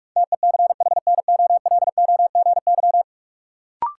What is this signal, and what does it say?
Morse code, heard as a keyed tone, about 42 wpm, 700 Hz. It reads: TECHNOLOGY
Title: Sentences from Top 1000 Words - Review Part 5 - New Word Drill - 45wpm